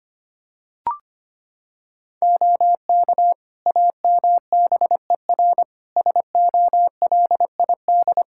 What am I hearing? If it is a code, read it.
Morse code, heard as a keyed tone, about 25 wpm, 700 Hz. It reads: OK AMBER SOLID
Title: Practice QSO - Exchange 3 - More Info - 25wpm